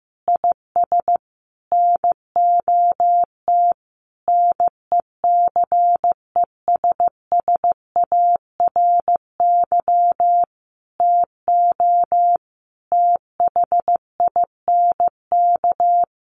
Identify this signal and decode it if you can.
Morse code, heard as a keyed tone, about 15 wpm, 700 Hz. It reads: IS NOT NECESSARY TO THINK
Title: Sentences from Top 1300 Words 15wpm